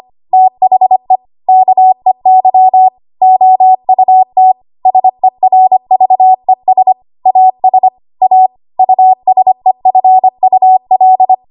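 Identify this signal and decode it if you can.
Morse code, heard as a keyed tone, about 25 wpm, 750 Hz. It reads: THE KEY OUT SERVES AS A USEFUL
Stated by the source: ARRL 25 WPM code practice archive.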